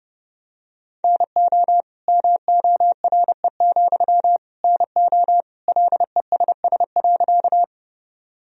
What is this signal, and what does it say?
Morse code, heard as a keyed tone, about 30 wpm, 700 Hz. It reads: NO MORE, NO LESS.
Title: Sentences from Top 1000 Words - Review Part 2 - 30wpm